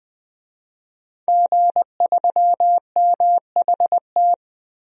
Morse code, heard as a keyed tone, about 20 wpm, 700 Hz. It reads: G3MHT
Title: DXCC-100 Call Signs - 5 Characters - Repeated 3x - 20wpm